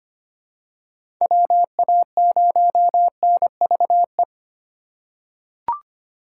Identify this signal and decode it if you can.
Morse code, heard as a keyed tone, about 25 wpm, 700 Hz. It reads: WA0NVE
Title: USA Call Signs - 4x Character Spacing - 25wpm